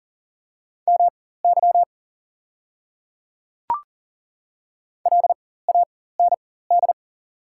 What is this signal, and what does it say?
Morse code, heard as a keyed tone, about 40 wpm, spaced as for 19 wpm, 690 Hz. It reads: MY LAND
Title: 4x Character Spacing - Top 200 Words - 40wpm